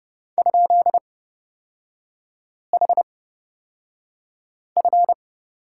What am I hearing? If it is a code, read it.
Morse code, heard as a keyed tone, about 30 wpm, spaced as for 5 wpm, 700 Hz. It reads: ?HF
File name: Lesson 36e - BT - Three Characters 30wpm